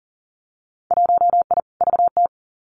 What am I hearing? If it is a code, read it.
Morse code, heard as a keyed tone, about 38 wpm, 700 Hz. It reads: 1I VT
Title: Winter Field Day Exchanges Encore 40wpm